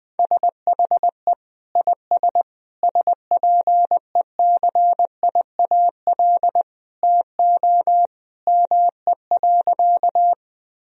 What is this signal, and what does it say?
Morse code, heard as a keyed tone, about 20 wpm, 700 Hz. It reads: SHE IS SPECIAL TO ME.
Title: Sentences from Top 1000 Words - Review Part 3 - 20wpm